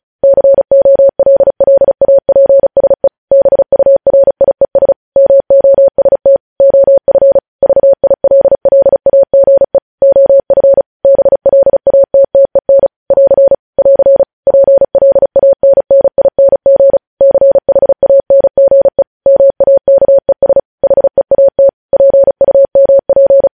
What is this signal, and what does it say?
Morse code, heard as a keyed tone, about 36 wpm, 550 Hz. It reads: COLLAPSE BURIES MOST OF VILLAGE OF BLATTEN <AR> <AR> PLANNING CHANGE MAKES HEAT PUMP
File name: podcast / News headlines-1x-35x35